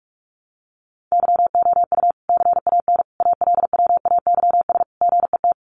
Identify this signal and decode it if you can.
Morse code, heard as a keyed tone, about 44 wpm, 700 Hz. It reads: YOU CAN ALWAYS GET
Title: Sentences from Top 1200 Words 45wpm